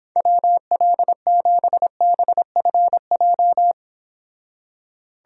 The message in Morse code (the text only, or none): WL7BFJ